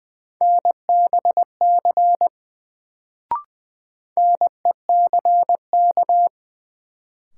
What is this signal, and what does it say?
Morse code, heard as a keyed tone, about 20 wpm, 700 Hz. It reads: NBC NECK
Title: Sets of 5 Words Encore - New Word Drill - 20wpm